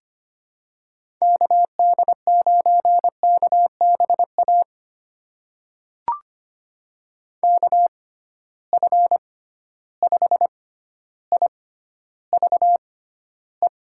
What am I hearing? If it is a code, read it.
Morse code, heard as a keyed tone, about 25 wpm, spaced as for 13 wpm, 700 Hz. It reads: KD9KBA KF5IVE
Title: USA Call Signs - 6 Characters - 6x Character Spacing - 25wpm